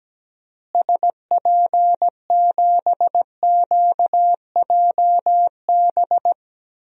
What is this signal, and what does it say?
Morse code, heard as a keyed tone, about 17 wpm, 700 Hz. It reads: SP7QJB
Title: DXCC-100 Call Signs - 6 Characters - Repeated 3x - 17wpm